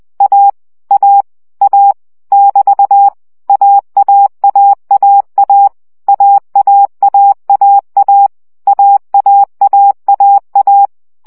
Morse code, heard as a keyed tone, about 20 wpm, 800 Hz. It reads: A A A <BT> AAAAA AAAAA AAAAA
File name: CW gyakorlas a 17 baud